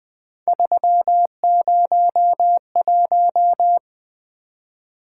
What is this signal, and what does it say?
Morse code, heard as a keyed tone, about 20 wpm, 700 Hz. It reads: 301
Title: Three Letter-Number Combinations - 20wpm